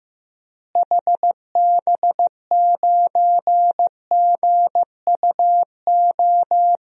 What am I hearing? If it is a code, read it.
Morse code, heard as a keyed tone, about 16 wpm, 700 Hz. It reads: HB9GUO
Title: DXCC-100 Call Signs - 6 Characters - Repeated 3x - 15wpm